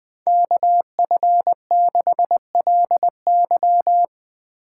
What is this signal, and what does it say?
Morse code, heard as a keyed tone, about 20 wpm, 700 Hz. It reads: KF6LY
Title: USA Call Signs - 5 Characters - 8x Character Spacing - 20wpm